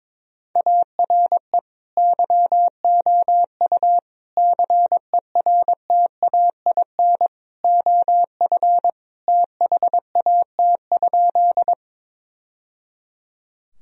Morse code, heard as a keyed tone, about 22 wpm, 700 Hz. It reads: ARE YOU CERTAIN OF THAT?
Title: Sentences from Top 1000 Words - Review Part 2 - 22wpm